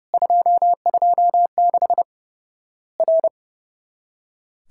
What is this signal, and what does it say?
Morse code, heard as a keyed tone, about 30 wpm, 700 Hz. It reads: 226 R
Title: Worked All Europe DX Contest - QTC Exchange - 30wpm